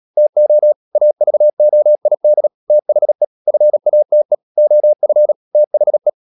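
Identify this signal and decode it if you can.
Morse code, heard as a keyed tone, about 37 wpm, 600 Hz. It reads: TO AVOID THE FATE OF THE
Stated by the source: Morse code practice at 35–39 wpm sending English text of Ken Follett's novel Never, from CW med Gnister.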